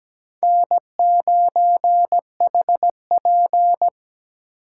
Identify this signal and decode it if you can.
Morse code, heard as a keyed tone, about 20 wpm, 700 Hz. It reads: N9HP
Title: Speed Racing - USA Call Signs 17wpm